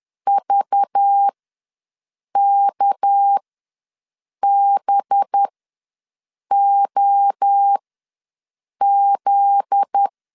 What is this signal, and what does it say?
Morse code, heard as a keyed tone, about 10 wpm, spaced as for 6 wpm, 800 Hz. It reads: VKBOZ
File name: LEZIONI / 7 GF9Z / 7 GF9Z ws 10wpm 3919